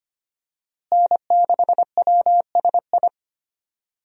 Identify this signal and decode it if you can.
Morse code, heard as a keyed tone, about 25 wpm, 700 Hz. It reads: N6WSI
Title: USA Call Signs - 5 Characters - Repeated 2x - 25wpm